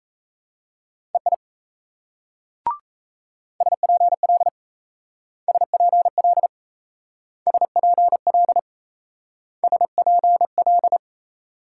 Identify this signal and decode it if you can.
Morse code, heard as a keyed tone, about 34 wpm, 700 Hz. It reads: EI SPL SPL SPL SPL
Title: Speed-Racing Multi-letter Phonemes - 28wpm